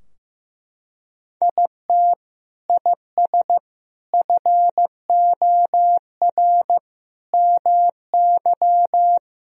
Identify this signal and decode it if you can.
Morse code, heard as a keyed tone, about 16 wpm, 700 Hz. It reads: IT IS FOR MY
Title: Sentences from Top 1000 Words - Review Part 4 - 15wpm